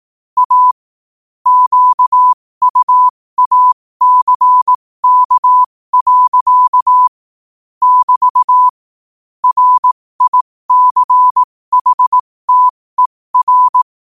Morse code, heard as a keyed tone, about 18 wpm, spaced as for 15 wpm, 1000 Hz. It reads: A QUACK. <BT> RICHTER
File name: qotdfiles / 15wpm